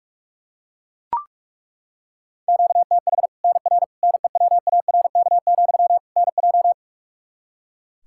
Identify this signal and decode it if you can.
Morse code, heard as a keyed tone, about 41 wpm, 700 Hz. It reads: QTH NR NEWARK, NJ
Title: Practice QSO - Exchange 1 - Beginning - 45wpm